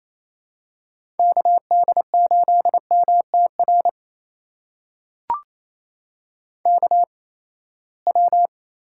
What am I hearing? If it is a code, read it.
Morse code, heard as a keyed tone, about 27 wpm, spaced as for 19 wpm, 700 Hz. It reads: KD8MTR KW
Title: USA Call Signs - 8x Character Spacing - 28wpm